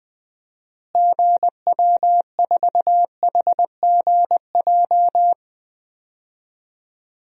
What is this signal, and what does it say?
Morse code, heard as a keyed tone, about 21 wpm, 700 Hz. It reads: GW4HGJ
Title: DXCC-100 Call Signs - 6 Characters - Repeated 2x - 20wpm